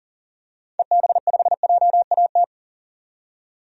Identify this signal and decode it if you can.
Morse code, heard as a keyed tone, about 40 wpm, 700 Hz. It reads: EB5JAT
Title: DXCC-100 Call Signs - 6 Characters - 40wpm